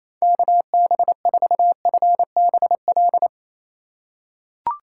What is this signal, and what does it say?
Morse code, heard as a keyed tone, about 27 wpm, 700 Hz. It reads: KB4FBL
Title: USA Call Signs - 3x Character Spacing - 28wpm